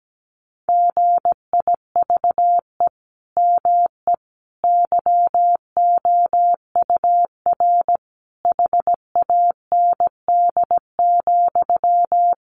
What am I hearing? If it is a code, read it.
Morse code, heard as a keyed tone, about 17 wpm, 700 Hz. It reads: GIVE ME YOUR HAND,
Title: Sentences from Top 1500 Words 17wpm